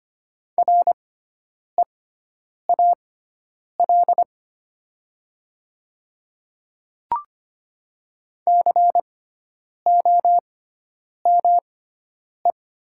Morse code, heard as a keyed tone, about 25 wpm, spaced as for 9 wpm, 690 Hz. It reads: REAL COME
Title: 6x Character Spacing - Top 300 Words - 25wpm